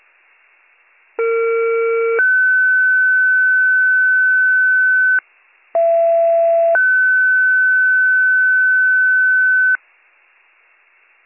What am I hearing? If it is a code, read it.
Two-tone page, heard as a one-second tone, then a three-second tone, about 460 then 1500 Hz, about 670 then 1500 Hz. It is heard from 1.2 to 9.8 s.